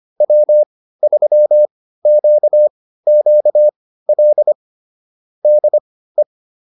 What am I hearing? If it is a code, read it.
Morse code, heard as a keyed tone, about 25 wpm, spaced as for 15 wpm, 600 Hz. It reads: W3QQL DE